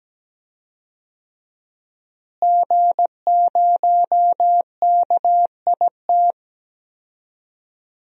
Morse code, heard as a keyed tone, about 17 wpm, 700 Hz. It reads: G0KIT
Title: DXCC-100 Call Signs - 5 Characters - Repeated 2x - 17wpm